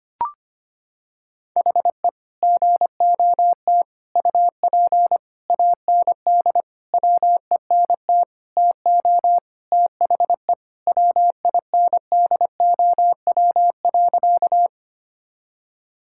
Morse code, heard as a keyed tone, about 25 wpm, 700 Hz. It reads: HE GOT UP AND WENT TO THE WINDOW.
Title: Sentences from Top 1600 Words 25wpm